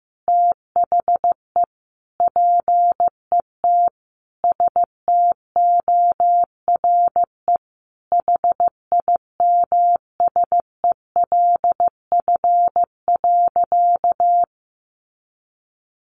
Morse code, heard as a keyed tone, about 15 wpm, 700 Hz. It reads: THE PET STORE HIMSELF.
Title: Easy Phrases 15wpm